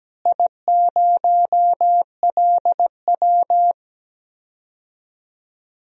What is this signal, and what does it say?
Morse code, heard as a keyed tone, about 17 wpm, 700 Hz. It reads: I0LW
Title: DXCC-100 Call Signs - 4 Characters - Repeated 3x - 17wpm